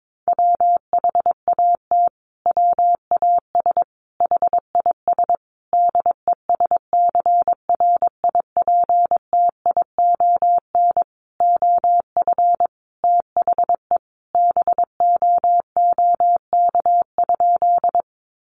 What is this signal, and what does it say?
Morse code, heard as a keyed tone, about 22 wpm, 700 Hz. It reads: WHAT WAS HIS DESCRIPTION OF THE BOOK?
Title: Sentences from Top 1300 Words 22wpm